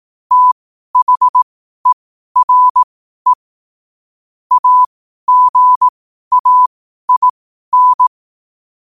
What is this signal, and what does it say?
Morse code, heard as a keyed tone, about 18 wpm, spaced as for 12 wpm, 1000 Hz. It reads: THERE AGAIN
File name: qotdfiles / 12wpm